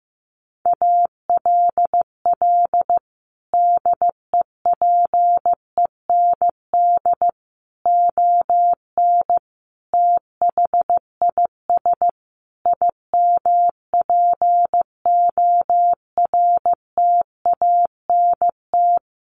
Morse code, heard as a keyed tone, about 15 wpm, 700 Hz. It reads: ALL DEPEND ON THIS IMPORTANT
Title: Sentences from Top 1300 Words 15wpm